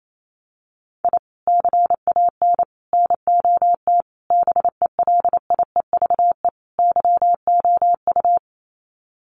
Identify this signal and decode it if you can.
Morse code, heard as a keyed tone, about 28 wpm, 700 Hz. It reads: I CAN NOT BELIEVE YOU
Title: Sets of 5 Words Encore 28wpm